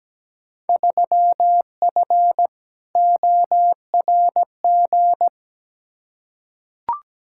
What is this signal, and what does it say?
Morse code, heard as a keyed tone, about 17 wpm, 700 Hz. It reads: 3F ORG
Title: Field Day Exchanges - 17wpm